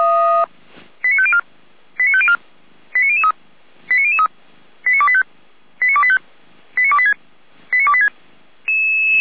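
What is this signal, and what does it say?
Five-tone selective call (ZVEI), heard as quick runs of five tones, tones of about 70 ms. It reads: 89504 89504 89003 89003 89285 89285 89286 89286
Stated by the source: ZVEI five-tone call received by SDR.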